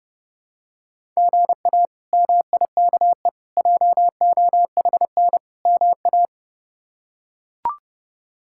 Morse code, heard as a keyed tone, about 30 wpm, 700 Hz. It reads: GA MIKE JOHN MA
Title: K1USN SST QSO Practice - 30wpm